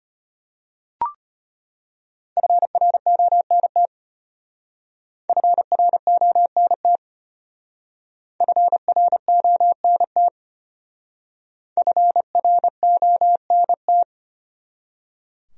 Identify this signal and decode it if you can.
Morse code, heard as a keyed tone, about 31 wpm, 700 Hz. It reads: FRONT FRONT FRONT FRONT
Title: Speed-Racing Top 500 Words 25wpm